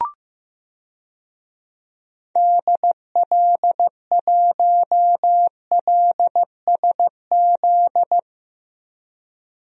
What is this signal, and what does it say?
Morse code, heard as a keyed tone, about 15 wpm, 700 Hz. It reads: DL1LSZ
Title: DXCC-100 Call Signs - 6 Characters - 15wpm